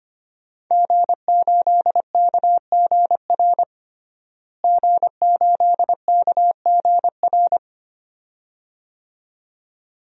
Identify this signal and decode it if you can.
Morse code, heard as a keyed tone, about 25 wpm, 700 Hz. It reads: G8KGR G8KGR
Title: DXCC-100 Call Signs - 5 Characters - Repeated 2x - 25wpm